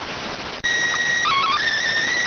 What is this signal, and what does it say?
SSTV (slow-scan television), PD 290.